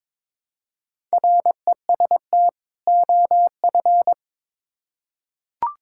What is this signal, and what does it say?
Morse code, heard as a keyed tone, about 21 wpm, 700 Hz. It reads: REST OF
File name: Build A Phrase - 2-Word Phrases - 22wpm